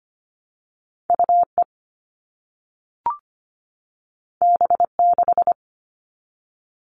Morse code, heard as a keyed tone, about 25 wpm, 700 Hz. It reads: UE B6
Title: All Cut Numbers - 2-Digit 25wpm